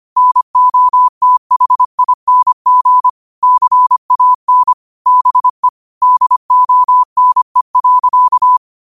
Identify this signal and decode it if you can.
Morse code, heard as a keyed tone, about 25 wpm, 1000 Hz. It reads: NOTHING CAN BE DONE.
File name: qotdfiles / 25wpm